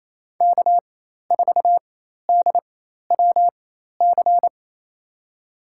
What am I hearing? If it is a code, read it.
Morse code, heard as a keyed tone, about 28 wpm, spaced as for 13 wpm, 700 Hz. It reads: K4DWC
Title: USA Call Signs - 5 Characters - 4x Character Spacing - 28wpm